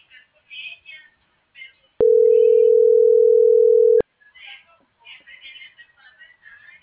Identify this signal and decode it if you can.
Ringback tone (North America), 440 + 480 Hz.